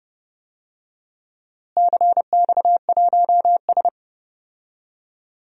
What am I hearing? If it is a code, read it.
Morse code, heard as a keyed tone, about 32 wpm, 700 Hz. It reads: CX1S
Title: DXCC-100 Call Signs - 4 Characters - Repeated 3x - 30wpm